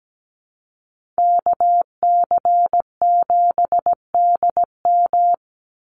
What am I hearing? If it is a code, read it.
Morse code, heard as a keyed tone, about 17 wpm, 700 Hz. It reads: KC7DM
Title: NA SOTA Activator Callsigns 17wpm